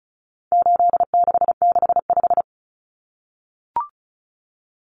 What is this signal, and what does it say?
Morse code, heard as a keyed tone, about 34 wpm, 700 Hz. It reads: 8665
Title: Numbers - 4-Digit 35wpm